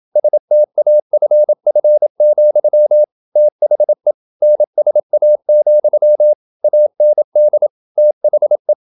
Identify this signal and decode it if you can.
Morse code, heard as a keyed tone, about 27 wpm, 600 Hz. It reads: STAFF, THE NSA, AND THE
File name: Never 0027wpm